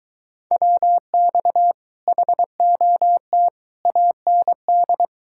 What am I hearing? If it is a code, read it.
Morse code, heard as a keyed tone, about 22 wpm, 700 Hz. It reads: WX HOT AND